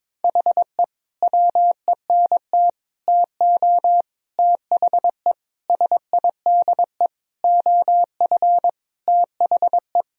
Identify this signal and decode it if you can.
Morse code, heard as a keyed tone, about 22 wpm, 700 Hz. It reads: HE WENT TO THE SIDE OF THE
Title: Sentences from Top 1000 Words - Review Part 1 - 22wpm